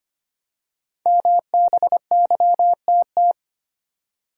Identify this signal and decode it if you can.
Morse code, heard as a keyed tone, about 25 wpm, 690 Hz. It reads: MBYTT
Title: Code Groups - 5 Letter:Numbers - 25wpm